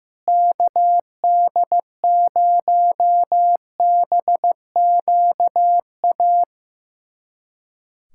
Morse code, heard as a keyed tone, about 15 wpm, 700 Hz. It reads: KD0BQA
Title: USA Callsign DE Callsign - 6 Characters - 15wpm